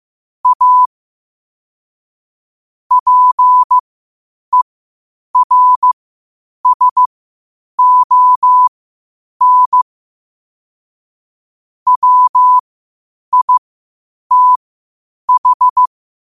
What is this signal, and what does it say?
Morse code, heard as a keyed tone, about 15 wpm, spaced as for 8 wpm, 1000 Hz. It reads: A PERSON WITH
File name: qotdfiles / 8wpm